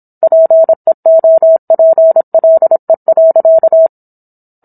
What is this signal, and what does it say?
Morse code, heard as a keyed tone, about 27 wpm, 650 Hz. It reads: PEOPLE.